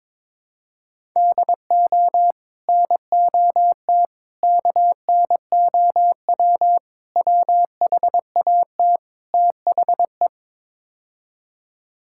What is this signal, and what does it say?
Morse code, heard as a keyed tone, about 22 wpm, 700 Hz. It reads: DO NOT KNOW WHAT THE
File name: practice / 22wpm / Build A Phrase - 5-Word Phrases - 22wpm